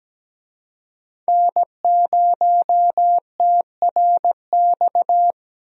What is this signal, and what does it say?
Morse code, heard as a keyed tone, about 17 wpm, 700 Hz. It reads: N0TRX
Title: USA Call Signs - 5 Characters - Repeated 2x - 17wpm